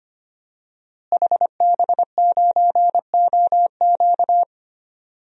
Morse code, heard as a keyed tone, about 25 wpm, 700 Hz. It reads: HB9OQ
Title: DXCC-100 Call Signs - 5 Characters - Repeated 3x - 25wpm